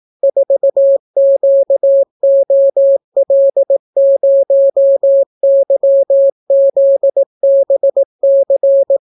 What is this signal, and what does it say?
Morse code, heard as a keyed tone, about 18 wpm, 550 Hz. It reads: 4QOL0YZBC